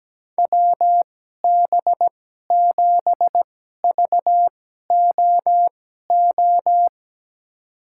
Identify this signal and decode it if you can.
Morse code, heard as a keyed tone, about 17 wpm, spaced as for 12 wpm, 700 Hz. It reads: WB7VOO
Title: USA Call Signs - 6 Characters - 2x Character Spacing - 17wpm